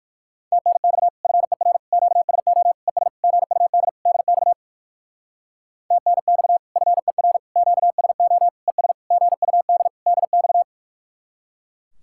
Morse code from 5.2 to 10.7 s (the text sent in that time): TNX FER QSO ES GUD DX